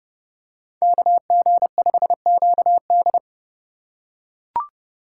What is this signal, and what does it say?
Morse code, heard as a keyed tone, about 29 wpm, 700 Hz. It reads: KG5QD